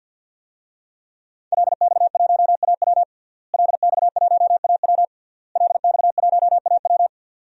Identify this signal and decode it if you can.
Morse code, heard as a keyed tone, about 46 wpm, 700 Hz. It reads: LX1AW LX1AW LX1AW
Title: DXCC-100 Call Signs - 5 Characters - Repeated 3x - 50wpm